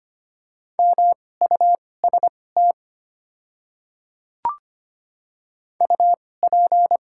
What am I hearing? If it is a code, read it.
Morse code, heard as a keyed tone, about 25 wpm, spaced as for 18 wpm, 700 Hz. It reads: MUST UP
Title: 2x Character Spacing - Top 500 Words - 25wpm